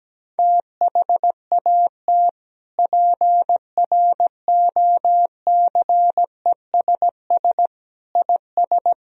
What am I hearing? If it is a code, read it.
Morse code, heard as a keyed tone, about 17 wpm, 700 Hz. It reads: THAT PROCESS IS
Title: Sentences from Top 1000 Words - Review Part 3 - 17wpm